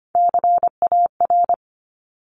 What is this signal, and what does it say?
Morse code, heard as a keyed tone, about 25 wpm, 700 Hz. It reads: CAR
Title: Top 500 Words Repeated 25wpm